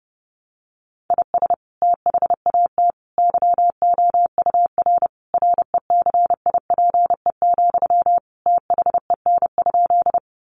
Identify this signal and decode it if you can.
Morse code, heard as a keyed tone, about 30 wpm, 700 Hz. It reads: IS THAT YOUR RECIPE, THEN?
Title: Sentences from Top 1200 Words 30wpm